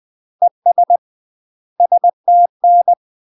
Morse code, heard as a keyed tone, about 20 wpm, 700 Hz. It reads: ES STN
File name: UK MASTER QSOs TRAINING / 20WPM / UK MASTER QSOs 0026